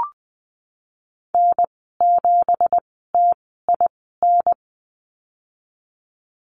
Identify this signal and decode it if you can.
Morse code, heard as a keyed tone, about 20 wpm, spaced as for 15 wpm, 700 Hz. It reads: N7TIN
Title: 173 Lesson 28g - Number 9 - Call Signs 20wpm